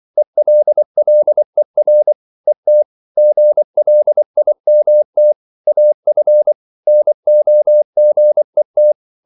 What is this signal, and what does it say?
Morse code, heard as a keyed tone, about 24 wpm, 600 Hz. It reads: ELLER ET GLIMT AF NOGET
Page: Ferie 24 wpm